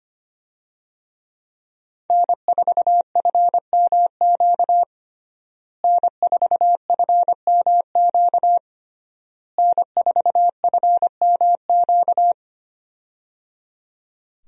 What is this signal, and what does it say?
Morse code, heard as a keyed tone, about 25 wpm, 700 Hz. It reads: N4FMQ N4FMQ N4FMQ
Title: USA Call Signs - 5 Characters - Repeated 3x - 25wpm